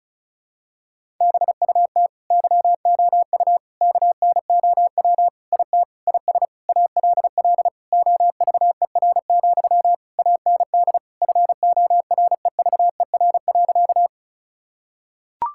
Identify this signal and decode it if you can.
Morse code, heard as a keyed tone, about 36 wpm, 700 Hz. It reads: BUT YOU KNOW IT IS ALL OVER, AND FOREVER.
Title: Sentences from Top 1600 Words 35wpm